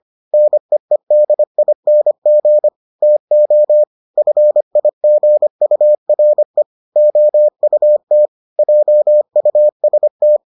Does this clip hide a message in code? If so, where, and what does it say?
Yes: Morse code from 0.3 to 10.4 s, heard as a keyed tone, about 25 wpm, 600 Hz. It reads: NEEDING TO FIGURE OUT JUST